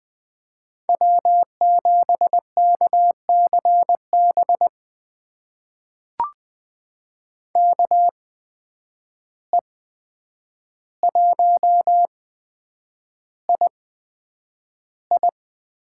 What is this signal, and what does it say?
Morse code, heard as a keyed tone, about 20 wpm, spaced as for 9 wpm, 700 Hz. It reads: W7KCB KE1II
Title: USA Call Signs - 5 Characters - 8x Character Spacing - 20wpm